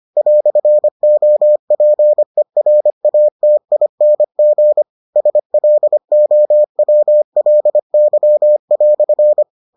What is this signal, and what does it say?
Morse code, heard as a keyed tone, about 25 wpm, 600 Hz. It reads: "OPERATING SLOWLY"